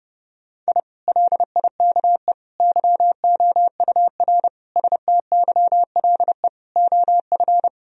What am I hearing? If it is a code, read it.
Morse code, heard as a keyed tone, about 30 wpm, 700 Hz. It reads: I LIKE YOUR STYLE OF
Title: Sentences from Top 1000 Words - Review Part 4 - 30wpm